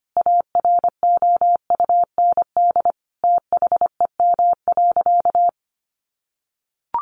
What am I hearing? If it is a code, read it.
Morse code, heard as a keyed tone, about 25 wpm, 700 Hz. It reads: AROUND THEM.
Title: Sentences from Top 600 Words 25wpm